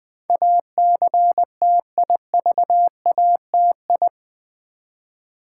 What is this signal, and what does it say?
Morse code, heard as a keyed tone, about 20 wpm, 700 Hz. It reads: ACTIVATI
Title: Build A Word - 4 Letters At A Time - 10-Letter Words - 20wpm